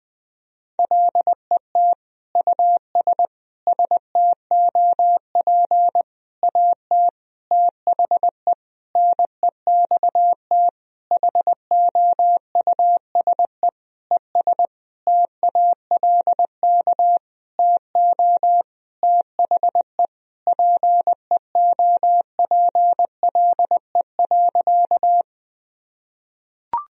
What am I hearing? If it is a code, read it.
Morse code, heard as a keyed tone, about 20 wpm, 700 Hz. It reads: LET US STOP AT THE NEXT HOUSE ES TALK TO THE PEOPLE.
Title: Sentences from Top 400 Words and Abbreviations 20wpm